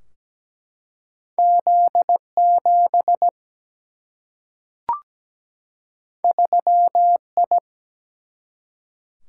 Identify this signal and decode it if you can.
Morse code, heard as a keyed tone, about 17 wpm, 700 Hz. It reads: Z7 3I
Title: Single-Letter Single-Number 17wpm